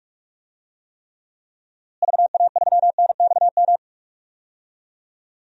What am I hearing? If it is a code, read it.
Morse code, heard as a keyed tone, about 46 wpm, 700 Hz. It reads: VA3NXM
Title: DXCC-100 Call Signs - 6 Characters - Repeated 2x - 45wpm